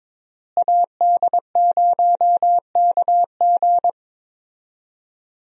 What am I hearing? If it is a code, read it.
Morse code, heard as a keyed tone, about 22 wpm, 700 Hz. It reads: AD0KG
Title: USA Call Signs - 5 Characters - Repeated 3x - 22wpm